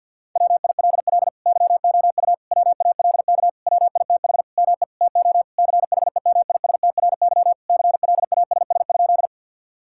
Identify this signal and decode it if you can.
Morse code, heard as a keyed tone, about 50 wpm, 700 Hz. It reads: WILL YOU WALK WITH ME TO CHEMISTRY CLASS?
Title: Sentences from Top 1000 Words - Review Part 6 - 50wpm